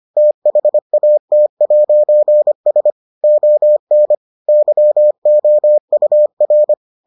Morse code, heard as a keyed tone, about 25 wpm, 600 Hz. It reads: THAT'S ON YOUR